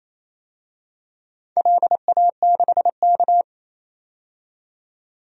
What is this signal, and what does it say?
Morse code, heard as a keyed tone, about 28 wpm, 700 Hz. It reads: LA6K
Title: DXCC-100 Call Signs - 4 Characters - 28wpm